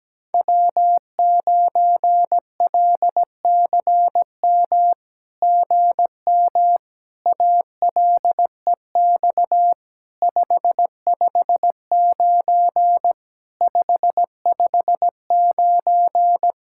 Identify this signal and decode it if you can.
Morse code, heard as a keyed tone, about 17 wpm, 700 Hz. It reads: W9LCM GM ALEX 559 559
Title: SOTA QSO Practice 17wpm